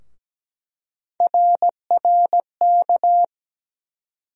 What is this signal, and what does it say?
Morse code, heard as a keyed tone, about 17 wpm, 700 Hz. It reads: RRK
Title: USA Call Sign Postfixes - 3 Letters 17wpm